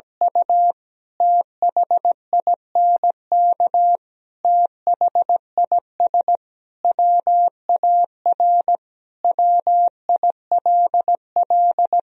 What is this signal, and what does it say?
Morse code, heard as a keyed tone, about 17 wpm, 700 Hz. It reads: U THINK THIS WAR WILL
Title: Sentences from Top 1000 Words - Review Part 2 - 17wpm